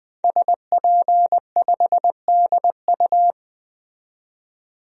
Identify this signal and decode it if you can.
Morse code, heard as a keyed tone, about 20 wpm, 700 Hz. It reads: SP5DU
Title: DXCC-100 Call Signs - 5 Characters - 20wpm